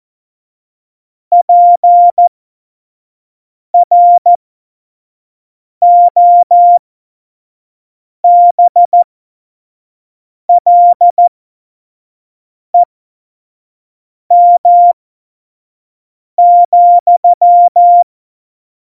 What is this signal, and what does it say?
Morse code, heard as a keyed tone, about 14 wpm, spaced as for 5 wpm, 700 Hz. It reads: PROBLEM,
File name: Morsekurs / Witze / Witz 05. Dezember 2017, 25 F70